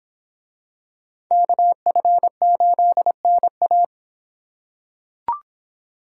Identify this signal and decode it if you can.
Morse code, heard as a keyed tone, about 25 wpm, 700 Hz. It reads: KF8NA